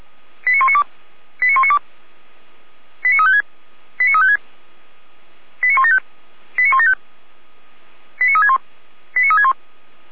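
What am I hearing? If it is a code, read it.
Five-tone selective call (ZVEI), heard as quick runs of five tones, tones of about 70 ms. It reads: 89192 89192 89356 89356 89175 89175 89361 89361
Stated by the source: ZVEI five-tone selective call received on an SDR.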